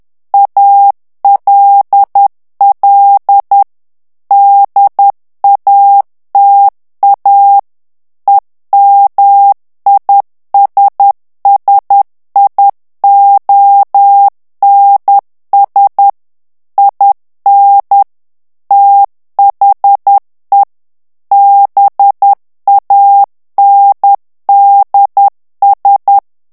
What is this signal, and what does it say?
Morse code, heard as a keyed tone, about 10 wpm, 800 Hz. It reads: ALL DATA EMISSIONS IN THE BANDS
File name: ANGLAIS texte clair 3 vitesse 10 mots minute